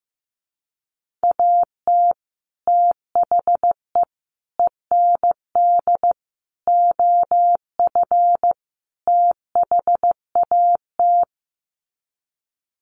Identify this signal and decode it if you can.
Morse code, heard as a keyed tone, about 15 wpm, 700 Hz. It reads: AT THE END OF THAT
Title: Sets of 5 Words Encore2 15wpm